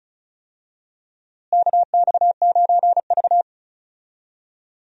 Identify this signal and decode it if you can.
Morse code, heard as a keyed tone, about 34 wpm, 700 Hz. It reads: KX9V
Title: USA Call Signs - 4 Characters - Repeated 2x - 35wpm